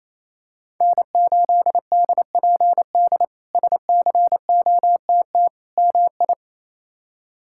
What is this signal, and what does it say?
Morse code, heard as a keyed tone, about 28 wpm, 700 Hz. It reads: N8DPD SCOTT MI
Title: K1USN SST QSO Practice - 28wpm